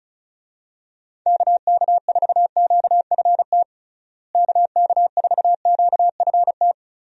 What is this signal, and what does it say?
Morse code, heard as a keyed tone, about 36 wpm, 700 Hz. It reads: KK4QFT KK4QFT
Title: USA Call Signs - 6 Characters - Repeated 3x - 35wpm